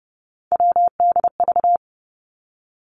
Morse code, heard as a keyed tone, about 30 wpm, 700 Hz. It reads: WDV
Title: Three Challenging Letter-Number Combinations 30wpm